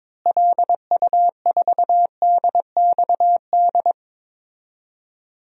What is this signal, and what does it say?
Morse code, heard as a keyed tone, about 22 wpm, 700 Hz. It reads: LU4DXD
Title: DXCC-100 Call Signs - 6 Characters - Repeated 3x - 22wpm